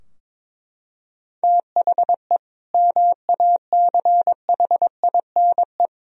Morse code, heard as a keyed tone, about 21 wpm, 700 Hz. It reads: THE MACHINE
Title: Sentences from Top 1000 Words - Review Part 2 - 22wpm